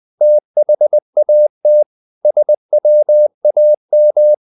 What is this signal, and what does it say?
Morse code, heard as a keyed tone, about 20 wpm, 600 Hz. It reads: THAT SWAM
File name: the old man and the sea / the old man and the sea part 1 20wpm